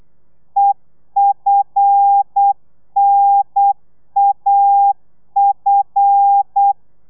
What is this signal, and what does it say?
Morse code, heard as a keyed tone, about 8 wpm, 800 Hz. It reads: EFNAF
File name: telegrafia / i2tfj / Lez5 35